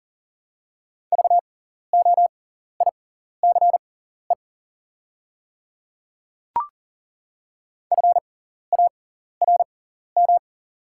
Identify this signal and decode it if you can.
Morse code, heard as a keyed tone, about 40 wpm, spaced as for 14 wpm, 690 Hz. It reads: VOICE FARM
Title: 6x Character Spacing - Top 500 Words - 40wpm